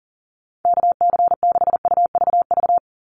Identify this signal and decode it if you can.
Morse code, heard as a keyed tone, about 39 wpm, 700 Hz. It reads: KC6UVV
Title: USA Call Signs 40wpm